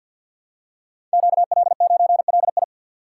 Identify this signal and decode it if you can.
Morse code, heard as a keyed tone, about 48 wpm, 700 Hz. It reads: QL9LI
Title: Code Groups - 5 Letter:Numbers - 50wpm